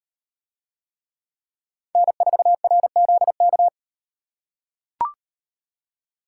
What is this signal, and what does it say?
Morse code, heard as a keyed tone, about 36 wpm, 700 Hz. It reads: N4RZK